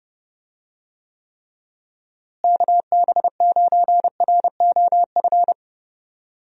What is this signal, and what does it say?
Morse code, heard as a keyed tone, about 32 wpm, 700 Hz. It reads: KB9ROF
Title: USA Call Signs - 6 Characters - Repeated 3x - 30wpm